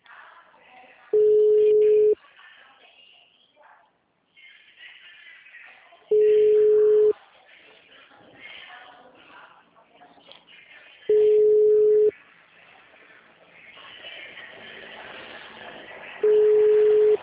Ringback tone (Europe, 425 Hz).
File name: out- -2004- - - .24843